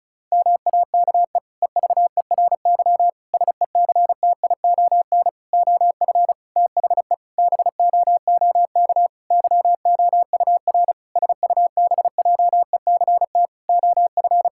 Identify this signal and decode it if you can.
Morse code, heard as a keyed tone, about 36 wpm, 700 Hz. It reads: MAKE EVERY SECTION OF THE BOOK YOUR SUBJECT OF
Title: Sentences from Top 1000 Words - Review Part 5 - 35wpm